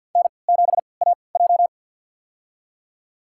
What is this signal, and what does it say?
Morse code, heard as a keyed tone, about 50 wpm, spaced as for 28 wpm, 700 Hz. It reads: N7AJ
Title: USA Call Signs - 4 Characters - 3x Character Spacing - 50wpm